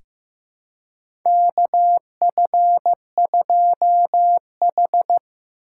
Morse code, heard as a keyed tone, about 15 wpm, 700 Hz. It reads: KF2H
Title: USA Call Signs - 4 Characters - 2x Character Spacing - 15wpm